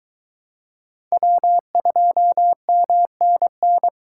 Morse code, heard as a keyed tone, about 22 wpm, 700 Hz. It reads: W2MNN